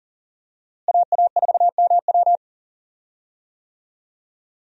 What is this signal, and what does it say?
Morse code, heard as a keyed tone, about 40 wpm, 700 Hz. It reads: AA4MW
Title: USA Call Signs - 5 Characters - Repeated 2x - 40wpm